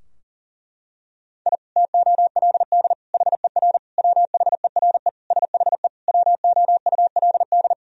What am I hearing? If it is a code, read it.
Morse code, heard as a keyed tone, about 37 wpm, 700 Hz. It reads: I TOLD HER WHERE SHE WOULD
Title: Sentences from Top 1000 Words - Review Part 2 - 40wpm